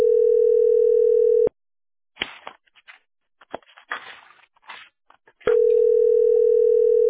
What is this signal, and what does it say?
Ringback tone (North America), 440 + 480 Hz.